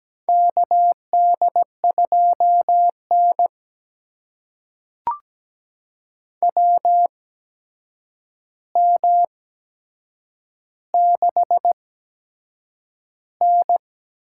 Morse code, heard as a keyed tone, about 17 wpm, spaced as for 7 wpm, 700 Hz. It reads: KD2N WM6N
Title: USA Call Signs - 4 Characters - 8x Character Spacing - 17wpm